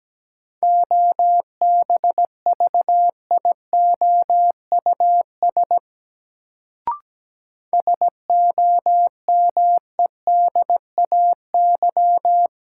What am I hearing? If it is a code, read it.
Morse code, heard as a keyed tone, about 17 wpm, 700 Hz. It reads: OBVIOUS SOMEDAY
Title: Sentences from Top 1600 Words - New Word Drill Alt - 17wpm